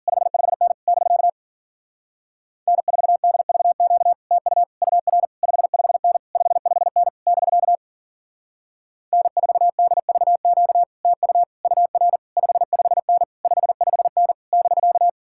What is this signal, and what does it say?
Morse code, heard as a keyed tone, about 50 wpm, 700 Hz. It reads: 55N <BK> N4DVQ TU UR 55N 55N <BK> N4DVQ TU UR 55N 55N <BK>